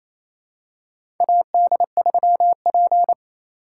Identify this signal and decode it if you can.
Morse code, heard as a keyed tone, about 28 wpm, 700 Hz. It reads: AD3P
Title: USA Call Signs - 4 Characters - 3x Character Spacing - 28wpm